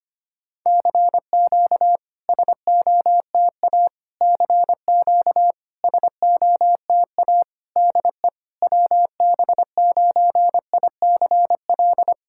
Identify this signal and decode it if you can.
Morse code, heard as a keyed tone, about 25 wpm, 690 Hz. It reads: CQ SOTA CQ SOTA DE WB9ICL
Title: SOTA QSO Practice 25wpm